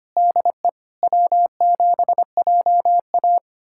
Morse code, heard as a keyed tone, about 25 wpm, 700 Hz. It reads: DE W7JA
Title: USA Callsign DE Callsign - 4 Characters - 25wpm